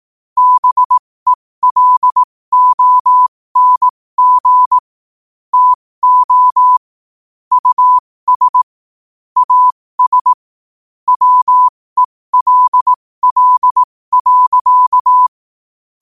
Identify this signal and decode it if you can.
Morse code, heard as a keyed tone, about 18 wpm, spaced as for 15 wpm, 1000 Hz. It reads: BELONG TO US AS WELL.
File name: qotdfiles / 15wpm